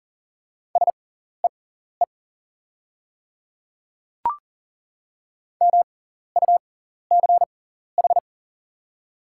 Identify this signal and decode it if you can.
Morse code, heard as a keyed tone, about 39 wpm, spaced as for 14 wpm, 700 Hz. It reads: SEE MUCH
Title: 6x Character Spacing - Top 200 Words - 40wpm